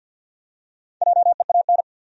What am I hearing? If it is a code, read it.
Morse code, heard as a keyed tone, about 52 wpm, 700 Hz. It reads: JEAN
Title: USA Common Names - Max 5 Letters - 50wpm